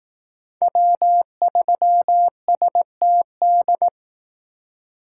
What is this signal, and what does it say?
Morse code, heard as a keyed tone, about 18 wpm, 700 Hz. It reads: W3STD
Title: Speed Racing - USA 1x3 Call Signs 15wpm